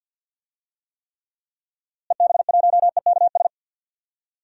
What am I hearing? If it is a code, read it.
Morse code, heard as a keyed tone, about 47 wpm, 700 Hz. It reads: EB1EKS
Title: DXCC-100 Call Signs - 6 Characters - Repeated 2x - 50wpm